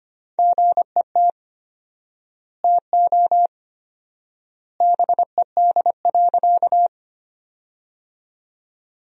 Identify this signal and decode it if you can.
Morse code, heard as a keyed tone, about 25 wpm, 700 Hz. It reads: GET TO BED.
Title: 4x Word-Spacing - Sentences from Top 500 Words - 25wpm